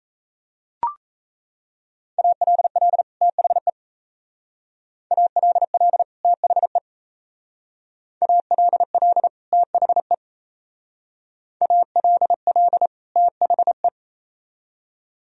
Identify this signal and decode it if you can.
Morse code, heard as a keyed tone, about 33 wpm, 690 Hz. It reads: ALL THE ALL THE ALL THE ALL THE
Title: Speed-Racing - Sets of 2 Words 28wpm